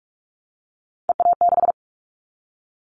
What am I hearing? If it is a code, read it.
Morse code, heard as a keyed tone, about 46 wpm, 700 Hz. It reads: EA6
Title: Callsign Prefixes for Top 100 DXCC - 45wpm